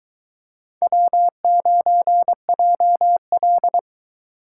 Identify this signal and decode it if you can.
Morse code, heard as a keyed tone, about 22 wpm, 700 Hz. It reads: W9JL